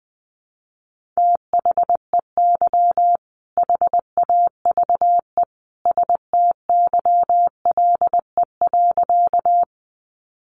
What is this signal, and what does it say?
Morse code, heard as a keyed tone, about 20 wpm, 700 Hz. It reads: THEY HAVE STYLE.
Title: Sentences from Top 800 Words 20wpm